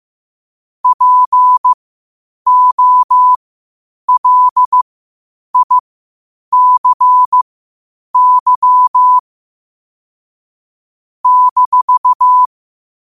Morse code, heard as a keyed tone, about 15 wpm, spaced as for 8 wpm, 1000 Hz. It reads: POLICY -
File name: qotdfiles / 8wpm